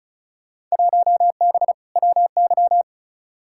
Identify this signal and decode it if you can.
Morse code, heard as a keyed tone, about 36 wpm, 700 Hz. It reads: 1B WY
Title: Field Day Exchanges Encore - 35wpm